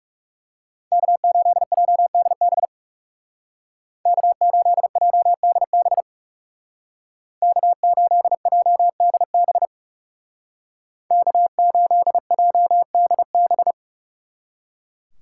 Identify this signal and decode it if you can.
Morse code, heard as a keyed tone, about 39 wpm, 700 Hz. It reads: K8JDB K8JDB K8JDB K8JDB